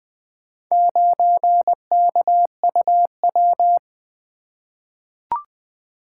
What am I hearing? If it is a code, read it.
Morse code, heard as a keyed tone, about 20 wpm, 700 Hz. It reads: 9KUW
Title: Four Letter-Number Combinations - 20wpm